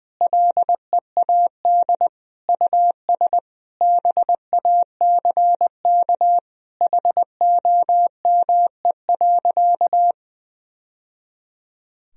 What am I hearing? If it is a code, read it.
Morse code, heard as a keyed tone, about 21 wpm, 700 Hz. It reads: LEAD US BACK HOME.
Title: Sentences from Top 1000 Words - Review Part 2 - 20wpm